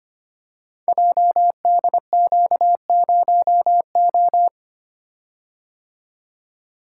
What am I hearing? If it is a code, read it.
Morse code, heard as a keyed tone, about 25 wpm, 700 Hz. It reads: JDQ0O
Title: Code Groups - 5 Letter:Numbers - 25wpm